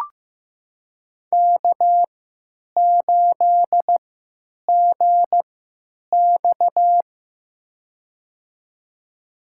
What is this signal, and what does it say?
Morse code, heard as a keyed tone, about 15 wpm, spaced as for 8 wpm, 700 Hz. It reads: K8GX
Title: USA Call Signs - 4 Characters - 3x Character Spacing - 15wpm